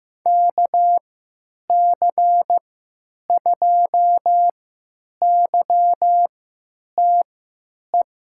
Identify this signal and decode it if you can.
Morse code, heard as a keyed tone, about 15 wpm, spaced as for 9 wpm, 700 Hz. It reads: KC2YTE
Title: USA Call Signs - 6 Characters - 3x Character Spacing - 15wpm